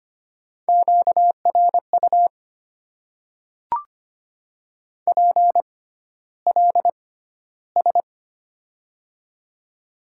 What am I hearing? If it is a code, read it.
Morse code, heard as a keyed tone, about 25 wpm, spaced as for 13 wpm, 700 Hz. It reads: QRU PLS
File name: Top 90 QSO Elements - 6x Character Spacing - 25wpm